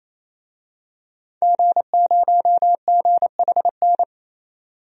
Morse code, heard as a keyed tone, about 28 wpm, 700 Hz. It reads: G0GHN
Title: DXCC-100 Call Signs - 5 Characters - 28wpm